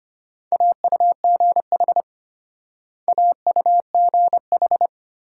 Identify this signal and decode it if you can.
Morse code, heard as a keyed tone, about 32 wpm, 700 Hz. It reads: AUGH AUGH